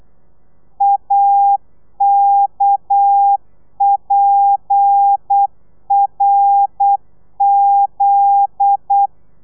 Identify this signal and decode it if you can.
Morse code, heard as a keyed tone, about 8 wpm, 800 Hz. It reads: AKPRZ
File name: telegrafia / i2tfj / Lez10 35